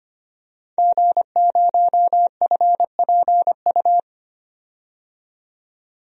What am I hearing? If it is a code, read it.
Morse code, heard as a keyed tone, about 25 wpm, 700 Hz. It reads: G0FPU
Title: DXCC-100 Call Signs - 5 Characters - 25wpm